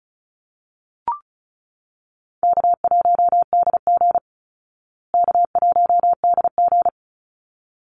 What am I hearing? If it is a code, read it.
Morse code, heard as a keyed tone, about 34 wpm, 700 Hz. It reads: K1DG K1DG
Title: Easy Call Signs Repeated 2x - 35wpm